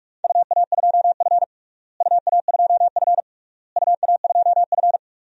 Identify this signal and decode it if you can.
Morse code, heard as a keyed tone, about 46 wpm, 700 Hz. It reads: UA2F UA2F UA2F
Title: DXCC-100 Call Signs - 4 Characters - Repeated 3x - 45wpm